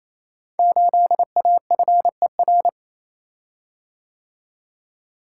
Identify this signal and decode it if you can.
Morse code, heard as a keyed tone, about 29 wpm, 700 Hz. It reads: 8AFER
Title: Code Groups - 5 Letter:Numbers - 28wpm